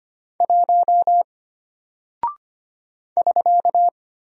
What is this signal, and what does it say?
Morse code, heard as a keyed tone, about 25 wpm, 700 Hz. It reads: 1 <SK>
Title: Single Character (Letter:Number:Punctuation:Prosign) Alt 25wpm